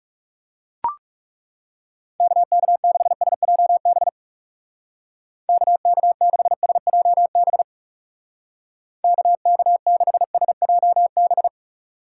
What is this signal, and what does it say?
Morse code, heard as a keyed tone, about 39 wpm, 700 Hz. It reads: KK6SJB KK6SJB KK6SJB